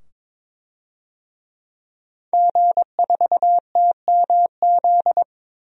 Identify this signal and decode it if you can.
Morse code, heard as a keyed tone, about 22 wpm, 700 Hz. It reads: G4TMZ
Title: DXCC-100 Call Signs - 5 Characters - 22wpm